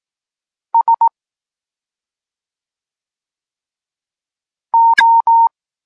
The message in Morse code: S O